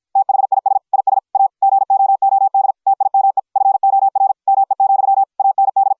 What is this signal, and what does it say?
Morse code, heard as a keyed tone, about 53 wpm, 750 Hz. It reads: THIS IS A GOOD TIME FOR ME, AND